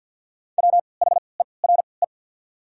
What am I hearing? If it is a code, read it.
Morse code, heard as a keyed tone, about 54 wpm, spaced as for 29 wpm, 700 Hz. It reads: WHERE
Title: 3x Character Spacing - Top 500 Words - 50wpm